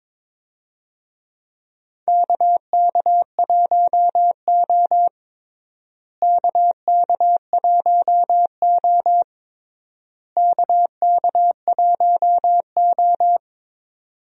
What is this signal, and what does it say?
Morse code, heard as a keyed tone, about 22 wpm, 700 Hz. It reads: KK1O KK1O KK1O
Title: USA Call Signs - 4 Characters - Repeated 3x - 22wpm